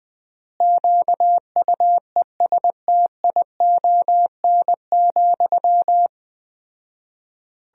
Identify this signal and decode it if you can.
Morse code, heard as a keyed tone, about 19 wpm, 700 Hz. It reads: QUESTION,
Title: 4x Word-Spacing - Sentences from Top 400 Words - 20wpm